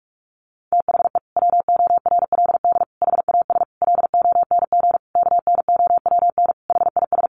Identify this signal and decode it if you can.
Morse code, heard as a keyed tone, about 44 wpm, 700 Hz. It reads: THE WORLD HAS LONG KNOWN HIS
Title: Sentences from Top 1200 Words 45wpm